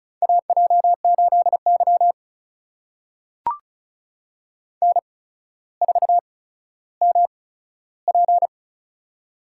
Morse code, heard as a keyed tone, about 36 wpm, spaced as for 15 wpm, 700 Hz. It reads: AJ8Y N4MP
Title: USA Call Signs - 4 Characters - 8x Character Spacing - 35wpm